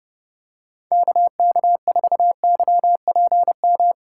Morse code, heard as a keyed tone, about 30 wpm, 700 Hz. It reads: KK4YPM
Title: USA Call Signs - 6 Characters - 4x Character Spacing - 30wpm